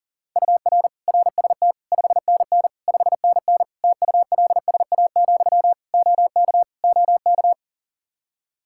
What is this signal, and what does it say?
Morse code, heard as a keyed tone, about 40 wpm, 700 Hz. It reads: UR RST 5NN 5NN TULSA, OK OK
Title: Practice QSO - Exchange 1 - Beginning - 40wpm